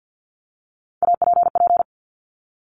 Morse code, heard as a keyed tone, about 50 wpm, 700 Hz. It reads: APP
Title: Top Three Letter Combinations in Words 50wpm